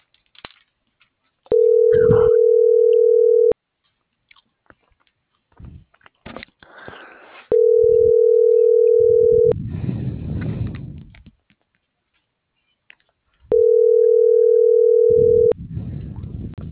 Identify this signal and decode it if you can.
Ringback tone (North America), 440 + 480 Hz.